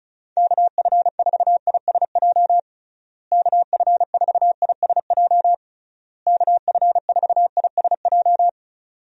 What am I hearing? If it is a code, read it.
Morse code, heard as a keyed tone, about 34 wpm, 700 Hz. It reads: KF4ISJ KF4ISJ KF4ISJ
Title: USA Call Signs - 6 Characters - Repeated 3x - 35wpm